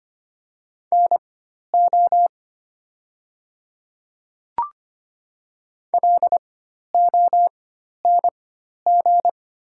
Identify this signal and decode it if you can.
Morse code, heard as a keyed tone, about 25 wpm, spaced as for 12 wpm, 700 Hz. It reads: NO LONG
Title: 4x Character Spacing - Top 100 Words - 25wpm